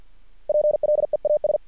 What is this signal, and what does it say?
Morse code, heard as a keyed tone, about 50 wpm, 600 Hz. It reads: PLEAS
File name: archived / CRSnet